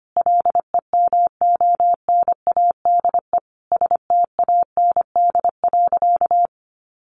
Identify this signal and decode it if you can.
Morse code, heard as a keyed tone, about 25 wpm, 700 Hz. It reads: LEMONADE STAND.
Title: Additional Easy Phrases 25wpm